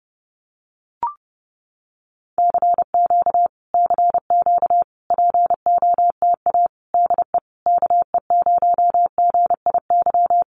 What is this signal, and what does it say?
Morse code, heard as a keyed tone, about 30 wpm, 700 Hz. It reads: CQ CQ POTA DE KE0GIY
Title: Common QSO Phrases Encore 30wpm